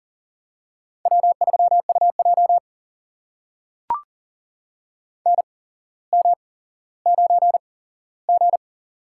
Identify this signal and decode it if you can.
Morse code, heard as a keyed tone, about 38 wpm, spaced as for 17 wpm, 700 Hz. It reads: W3UJ NM9G
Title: USA Call Signs - 4 Characters - 8x Character Spacing - 40wpm